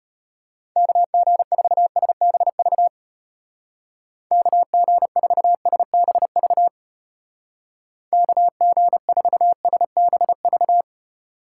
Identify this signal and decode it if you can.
Morse code, heard as a keyed tone, about 31 wpm, 700 Hz. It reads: KG4SBV KG4SBV KG4SBV
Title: Speed Racing - USA 2x3 Call Signs 25wpm